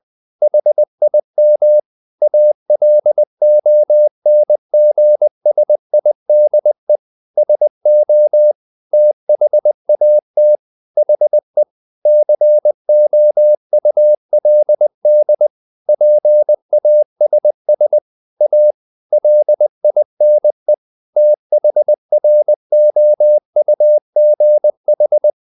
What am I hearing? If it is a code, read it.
Morse code, heard as a keyed tone, about 20 wpm, 600 Hz. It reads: HIM ALONGSIDE SO THAT HE COULD PASS A LINE THROUGH
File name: the old man and the sea part 2 20wpm